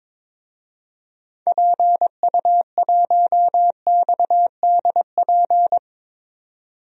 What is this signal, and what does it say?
Morse code, heard as a keyed tone, about 22 wpm, 700 Hz. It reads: PU1XDP
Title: DXCC-100 Call Signs - 6 Characters - 22wpm